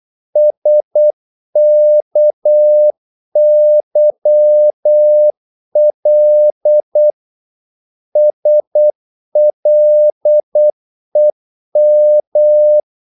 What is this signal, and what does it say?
Morse code, heard as a keyed tone, about 8 wpm, 600 Hz. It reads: SKYL SLEM
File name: Korte ord DK 8wpm